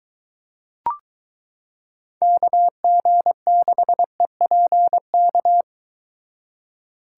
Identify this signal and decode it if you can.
Morse code, heard as a keyed tone, about 23 wpm, 700 Hz. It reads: KG6EPK